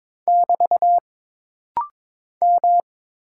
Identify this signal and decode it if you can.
Morse code, heard as a keyed tone, about 22 wpm, 700 Hz. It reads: <BT> M
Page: This rapid-fire timing is geared to push your ICR (Instant Character Recognition) skills.